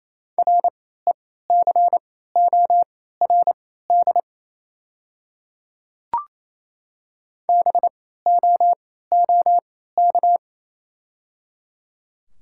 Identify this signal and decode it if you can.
Morse code, heard as a keyed tone, about 27 wpm, spaced as for 16 wpm, 690 Hz. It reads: RECORD BOOK
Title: 3x Character Spacing - Top 500 Words - 28wpm